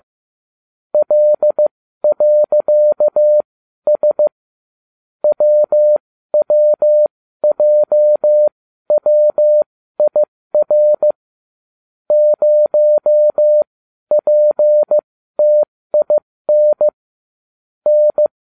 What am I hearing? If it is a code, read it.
Morse code, heard as a keyed tone, about 15 wpm, spaced as for 12 wpm, 600 Hz. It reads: L.S WWJWIR 0PTIN N